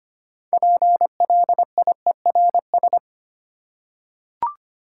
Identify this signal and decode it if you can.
Morse code, heard as a keyed tone, about 25 wpm, 700 Hz. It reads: PLIERS
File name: practice / 25wpm / Easy Words with Hints - 25wpm